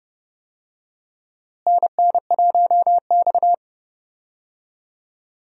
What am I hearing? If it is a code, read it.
Morse code, heard as a keyed tone, about 30 wpm, 700 Hz. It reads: NN1X
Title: USA Call Signs - 4 Characters - Repeated 2x - 30wpm